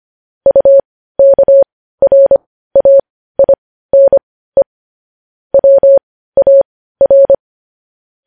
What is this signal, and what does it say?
Morse code, heard as a keyed tone, about 25 wpm, spaced as for 15 wpm, 550 Hz. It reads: UKRAINE WAR
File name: podcast / News headlines-1x-25x15